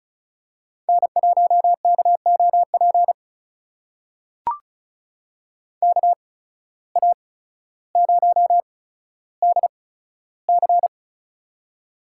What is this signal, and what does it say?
Morse code, heard as a keyed tone, about 36 wpm, spaced as for 15 wpm, 700 Hz. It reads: N1KOP KA0DC
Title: USA Call Signs - 5 Characters - 8x Character Spacing - 35wpm